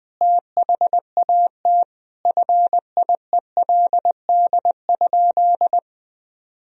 Morse code, heard as a keyed tone, about 20 wpm, 700 Hz. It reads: THAT FIELD?
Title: Sentences from Top 1000 Words - Review Part 2 - 20wpm